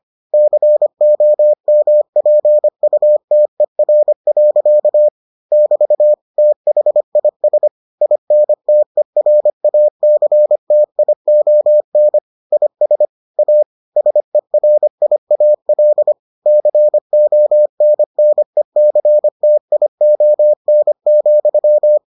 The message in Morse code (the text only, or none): COMPUTER. <BT> THIS INTERACTION IS A SERIAL CONNECTION,